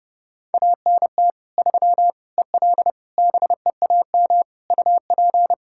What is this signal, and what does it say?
Morse code, heard as a keyed tone, about 30 wpm, 700 Hz. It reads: ANT 3 EL BEAM UP
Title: Practice QSO - Exchange 2 - Info - 30wpm